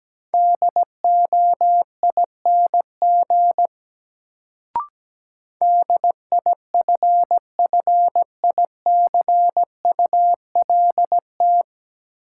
Morse code, heard as a keyed tone, about 17 wpm, 700 Hz. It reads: DOING DIFFICULT
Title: Sets of 4 Words - New Word Drill - 17wpm